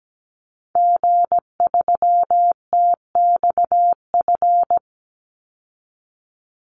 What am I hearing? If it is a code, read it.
Morse code, heard as a keyed tone, about 17 wpm, 700 Hz. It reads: G3TXF
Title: Call Signs Repeated 3x - 17wpm